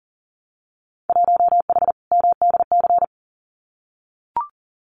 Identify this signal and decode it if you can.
Morse code, heard as a keyed tone, about 41 wpm, 700 Hz. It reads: 1H MDC
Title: Winter Field Day Exchanges 40wpm